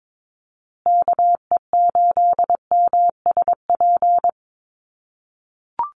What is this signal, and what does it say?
Morse code, heard as a keyed tone, about 22 wpm, 700 Hz. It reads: KE8MSP
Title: USA Call Signs Repeated 2x 22wpm